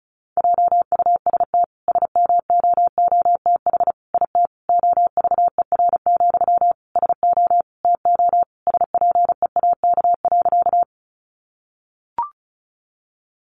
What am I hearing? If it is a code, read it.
Morse code, heard as a keyed tone, about 34 wpm, 700 Hz. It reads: JUST SMOOTH IT OVER, SO TO SPEAK.
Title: Sentences from Top 1400 Words 35wpm